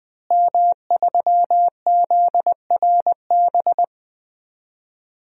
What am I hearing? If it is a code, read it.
Morse code, heard as a keyed tone, about 20 wpm, 700 Hz. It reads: M3ZRB
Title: DXCC-100 Call Signs - 5 Characters - Repeated 3x - 20wpm